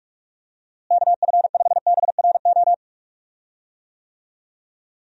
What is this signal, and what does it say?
Morse code, heard as a keyed tone, about 46 wpm, 700 Hz. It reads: KF5BRO
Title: USA Call Signs - 6 Characters - Repeated 3x - 45wpm